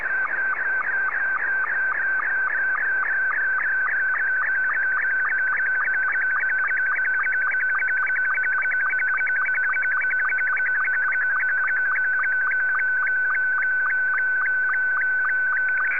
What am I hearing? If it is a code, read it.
SSTV (slow-scan television), Scottie 2.